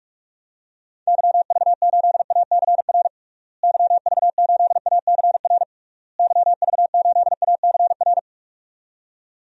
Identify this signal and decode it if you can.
Morse code, heard as a keyed tone, about 46 wpm, 700 Hz. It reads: YV8ACR YV8ACR YV8ACR
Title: DXCC-100 Call Signs - 6 Characters - Repeated 3x - 45wpm